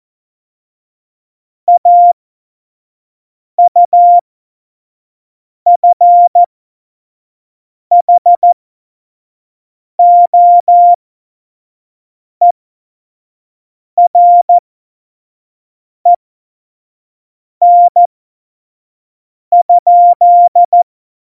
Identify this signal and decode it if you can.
Morse code, heard as a keyed tone, about 14 wpm, spaced as for 5 wpm, 700 Hz. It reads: AUFHOEREN?
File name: Morsekurs / Witze / Witz 07. November 2017, 25 F70